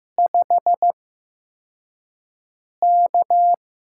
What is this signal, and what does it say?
Morse code, heard as a keyed tone, about 15 wpm, spaced as for 4 wpm, 700 Hz. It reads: 5K
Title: USA Call Signs - 6 Characters - 8x Character Spacing - 15wpm